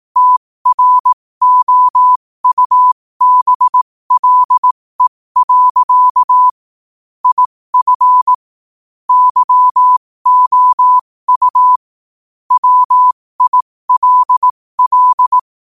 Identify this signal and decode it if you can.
Morse code, heard as a keyed tone, about 18 wpm, spaced as for 15 wpm, 1000 Hz. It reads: TROUBLE. IF YOU WILL
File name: qotdfiles / 15wpm